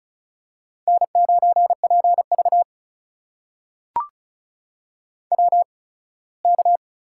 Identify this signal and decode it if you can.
Morse code, heard as a keyed tone, about 36 wpm, spaced as for 21 wpm, 700 Hz. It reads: N9PV WK
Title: USA Call Signs - 4 Characters - 8x Character Spacing - 35wpm